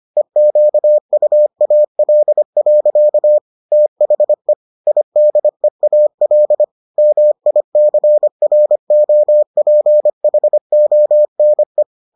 Morse code, heard as a keyed tone, about 25 wpm, 600 Hz. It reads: EQUAL. THE IDEAL MICROPHONE